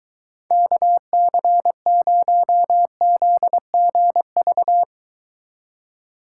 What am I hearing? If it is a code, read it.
Morse code, heard as a keyed tone, about 23 wpm, 700 Hz. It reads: KC0ZGV